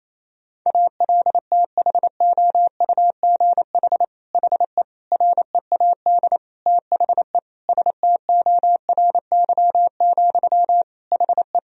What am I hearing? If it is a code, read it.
Morse code, heard as a keyed tone, about 27 wpm, 700 Hz. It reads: ALTHOUGH HE READ THE STORY, HE
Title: Sentences from Top 1000 Words - Review Part 3 - 28wpm